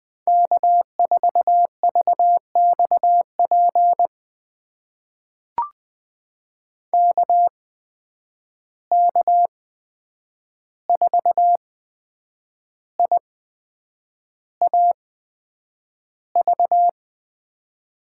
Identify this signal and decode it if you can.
Morse code, heard as a keyed tone, about 20 wpm, spaced as for 8 wpm, 700 Hz. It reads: K4VXP KK4IAV
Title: USA Call Signs - 8x Character Spacing - 20wpm